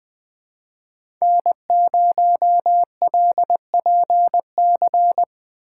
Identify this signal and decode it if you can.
Morse code, heard as a keyed tone, about 20 wpm, 700 Hz. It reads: N0LPC
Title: Speed Racing - USA 1x3 Call Signs 17wpm